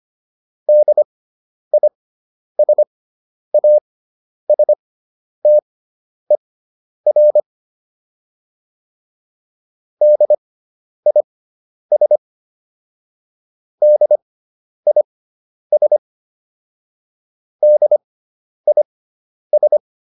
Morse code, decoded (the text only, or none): DISASTER DIS DIS DIS